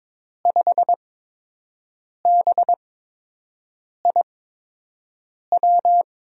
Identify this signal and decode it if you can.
Morse code, heard as a keyed tone, about 22 wpm, spaced as for 6 wpm, 700 Hz. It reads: 5BIW
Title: USA Call Signs - 6 Characters - 8x Character Spacing - 22wpm